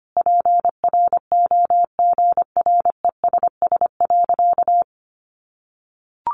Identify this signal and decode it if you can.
Morse code, heard as a keyed tone, about 25 wpm, 700 Hz. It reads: PROGRESS.
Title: Sentences from Top 1400 Words 25wpm